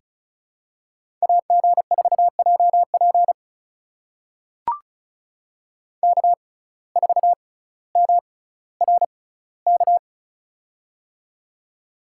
Morse code, heard as a keyed tone, about 36 wpm, spaced as for 18 wpm, 700 Hz. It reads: AG4JP K4MRK
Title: USA Call Signs - 5 Characters - 6x Character Spacing - 35wpm